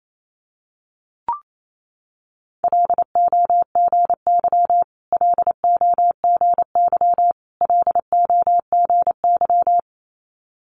Morse code, heard as a keyed tone, about 28 wpm, 700 Hz. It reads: LOGY LOGY LOGY
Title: Multi-letter Phonemes Repeated 3x 28wpm